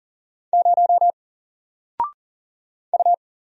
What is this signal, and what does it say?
Morse code, heard as a keyed tone, about 40 wpm, 700 Hz. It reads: 0 U
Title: Single Letter-Number Alt 40wpm